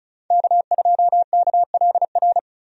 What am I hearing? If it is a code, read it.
Morse code, heard as a keyed tone, about 34 wpm, 700 Hz. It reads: K2KLR
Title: USA Call Signs - 5 Characters - 35wpm